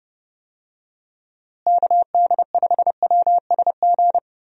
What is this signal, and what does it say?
Morse code, heard as a keyed tone, about 30 wpm, 700 Hz. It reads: KD5WSG
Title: USA Call Signs - 6 Characters - Repeated 3x - 30wpm